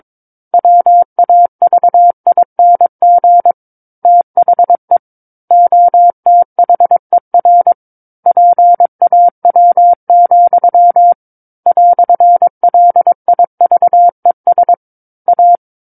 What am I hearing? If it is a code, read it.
Morse code, heard as a keyed tone, about 22 wpm, 700 Hz. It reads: WAVING THE OTHER PAW, "LIVES A
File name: 06-Chapter-06 22 WPM